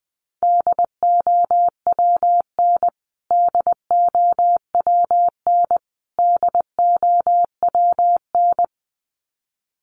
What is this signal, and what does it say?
Morse code, heard as a keyed tone, about 19 wpm, 700 Hz. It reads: DOWN DOWN DOWN
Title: Top 200 Words Repeated 20wpm